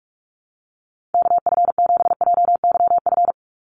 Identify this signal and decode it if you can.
Morse code, heard as a keyed tone, about 44 wpm, 700 Hz. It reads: KF7JYF
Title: USA Call Signs 45wpm